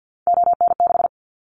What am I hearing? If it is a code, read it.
Morse code, heard as a keyed tone, about 50 wpm, 700 Hz. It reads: ON6
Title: Callsign Prefixes for Top 25 DXCC - Repeated 3x - 50wpm